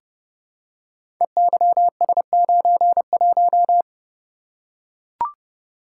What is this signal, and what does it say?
Morse code, heard as a keyed tone, about 30 wpm, 700 Hz. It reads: EYS91
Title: Code Groups - 5 Letter:Numbers - 30wpm